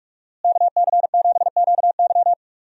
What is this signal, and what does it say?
Morse code, heard as a keyed tone, about 42 wpm, 700 Hz. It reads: KC7QY
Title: USA Call Signs - 5 Characters - Repeated 2x - 45wpm